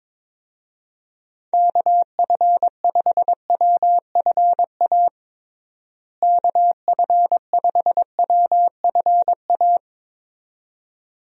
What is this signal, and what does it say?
Morse code, heard as a keyed tone, about 22 wpm, 700 Hz. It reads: KF5WFA KF5WFA
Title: USA Call Signs - 6 Characters - Repeated 2x - 22wpm